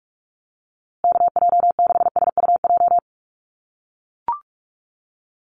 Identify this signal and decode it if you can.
Morse code, heard as a keyed tone, about 44 wpm, 700 Hz. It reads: KJ6SUJ
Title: USA Call Signs Repeated 2x 45wpm